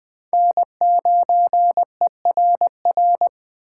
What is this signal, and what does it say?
Morse code, heard as a keyed tone, about 24 wpm, 700 Hz. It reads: N9ERR
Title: Speed Racing - USA Call Signs 20wpm